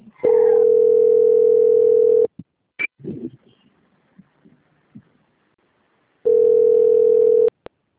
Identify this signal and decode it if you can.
Ringback tone (North America), 440 + 480 Hz.